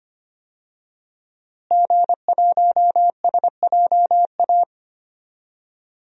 Morse code, heard as a keyed tone, about 25 wpm, 700 Hz. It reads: G1SJA
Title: DXCC-100 Call Signs - 5 Characters - Repeated 2x - 25wpm